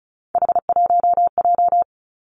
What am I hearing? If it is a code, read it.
Morse code, heard as a keyed tone, about 36 wpm, 700 Hz. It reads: H1J
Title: Three Challenging Letter-Number Combinations 35wpm